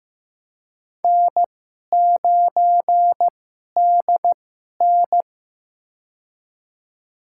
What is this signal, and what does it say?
Morse code, heard as a keyed tone, about 15 wpm, spaced as for 11 wpm, 700 Hz. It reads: N9DN
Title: USA Call Signs - 2x Character Spacing - 15wpm